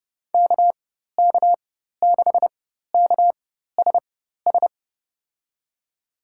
Morse code, heard as a keyed tone, about 31 wpm, spaced as for 14 wpm, 700 Hz. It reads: KK6KSS
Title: USA Call Signs - 6 Characters - 4x Character Spacing - 30wpm